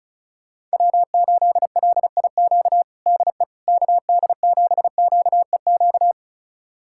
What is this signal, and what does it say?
Morse code, heard as a keyed tone, about 36 wpm, 700 Hz. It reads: W8LIQ DE KD7QEQ
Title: USA Callsign DE Callsign 35wpm